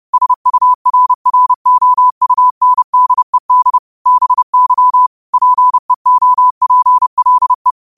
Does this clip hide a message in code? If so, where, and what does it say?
Yes: Morse code from 0.1 to 7.7 s, heard as a keyed tone, about 30 wpm, 1000 Hz. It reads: SURROUNDED BY PEOPLE